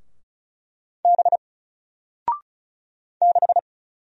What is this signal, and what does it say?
Morse code, heard as a keyed tone, about 35 wpm, 700 Hz. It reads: B 6
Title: Single Character (Letter:Number:Punctuation:Prosign) 35wpm